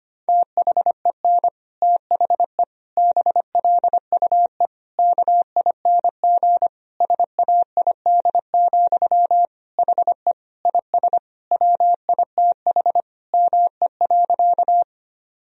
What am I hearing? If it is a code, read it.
Morse code, heard as a keyed tone, about 25 wpm, 700 Hz. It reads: THEN THE BLUE KING SAID, HE IS WITH ME.
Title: Sentences from Top 1000 Words - Review Part 3 - 25wpm